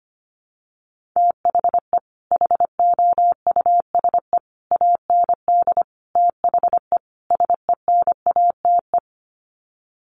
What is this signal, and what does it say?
Morse code, heard as a keyed tone, about 25 wpm, 700 Hz. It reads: THE HOUSE AND THE SENATE
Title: Sets of 5 Words Encore2 25wpm